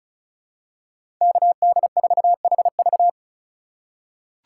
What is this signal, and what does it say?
Morse code, heard as a keyed tone, about 34 wpm, 700 Hz. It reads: KD4HV
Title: USA Call Signs - 5 Characters - 4x Character Spacing - 35wpm